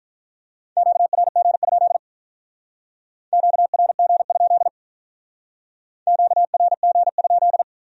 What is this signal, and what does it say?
Morse code, heard as a keyed tone, about 45 wpm, 700 Hz. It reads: QRG? QRG? QRG?